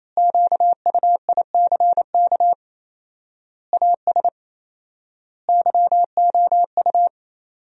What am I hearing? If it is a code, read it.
Morse code, heard as a keyed tone, about 28 wpm, 700 Hz. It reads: QUICK AS YOU
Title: 4x Word-Spacing - Sentences from Top 500 Words - 28wpm